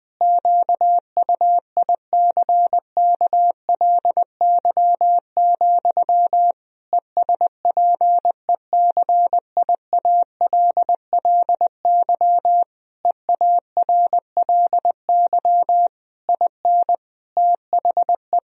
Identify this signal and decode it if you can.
Morse code, heard as a keyed tone, about 20 wpm, 700 Hz. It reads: QUICKLY, ESPECIALLY EARLY IN THE
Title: Sentences from Top 1000 Words - Review Part 4 - 20wpm